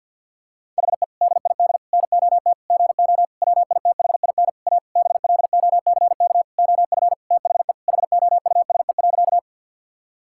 Morse code, heard as a keyed tone, about 50 wpm, 700 Hz. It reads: HE DID NOT GO WITHIN A BLOCK OF THE HOUSE.
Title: Sentences from Top 1600 Words 50wpm